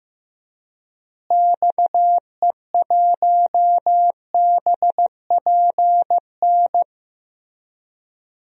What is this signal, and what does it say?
Morse code, heard as a keyed tone, about 16 wpm, 700 Hz. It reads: XE1BPN
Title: DXCC-100 Call Signs - 6 Characters - Repeated 2x - 15wpm